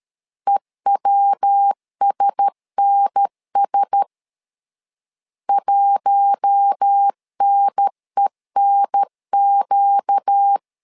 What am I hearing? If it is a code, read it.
Morse code, heard as a keyed tone, about 13 wpm, 800 Hz. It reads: EWSNS 1NENQ
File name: LEZIONI / 4 Q1S 12wpm 1802